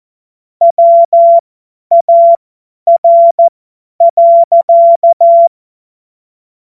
Morse code, heard as a keyed tone, about 14 wpm, spaced as for 10 wpm, 680 Hz. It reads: WAR.
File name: Morsekurs / Witze / Witz 18. April 2023 50 F70